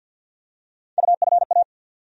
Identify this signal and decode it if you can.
Morse code, heard as a keyed tone, about 47 wpm, 700 Hz. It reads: UFA
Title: USA Call Sign Postfixes - 3 Letters 50wpm